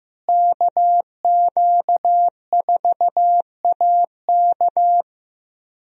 Morse code, heard as a keyed tone, about 15 wpm, 700 Hz. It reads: KQ4AK
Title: USA Call Signs - 5 Characters - Repeated 2x - 15wpm